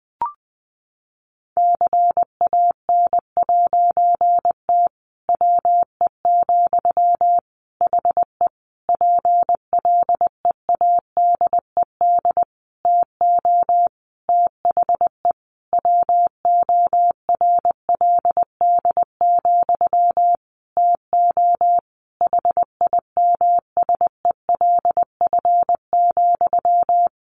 Morse code, heard as a keyed tone, about 20 wpm, 700 Hz. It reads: CAN'T WE, HE PLEADED TO THE WORLD, TO HIMSELF,
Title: A Sound Of Thunder 20wpm